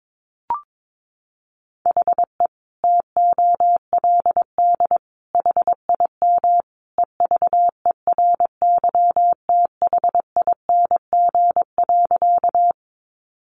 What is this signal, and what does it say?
Morse code, heard as a keyed tone, about 22 wpm, 700 Hz. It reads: HE TOLD HIM EVERYTHING.
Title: Sentences from Top 700 Words 22wpm